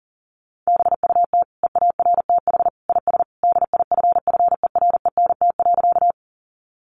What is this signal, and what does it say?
Morse code, heard as a keyed tone, about 37 wpm, 700 Hz. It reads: BUT EARTH IS DIFFERENT.
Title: Sentences from Top 600 Words 40wpm